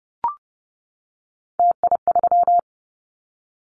Morse code, heard as a keyed tone, about 30 wpm, 700 Hz. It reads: TI3
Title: Callsign Prefixes for Top 100 DXCC - 30wpm